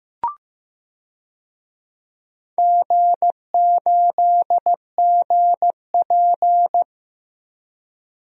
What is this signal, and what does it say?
Morse code, heard as a keyed tone, about 15 wpm, 700 Hz. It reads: G8GP
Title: DXCC-100 Call Signs - 4 Characters - Repeated 2x - 15wpm